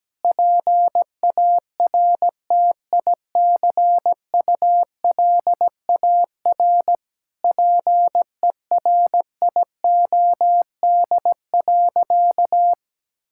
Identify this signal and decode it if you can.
Morse code, heard as a keyed tone, about 17 wpm, 700 Hz. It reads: PARTICULAR PERIOD.
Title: Sentences from Top 1000 Words - Review Part 5 - 17wpm